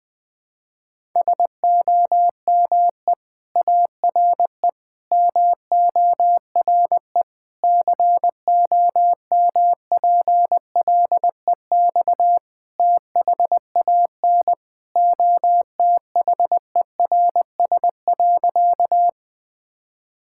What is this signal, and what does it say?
Morse code, heard as a keyed tone, about 20 wpm, 700 Hz. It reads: SOME ARE MORE COMPLEX THAN OTHERS.
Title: Sentences from Top 1000 Words - Review Part 6 - 20wpm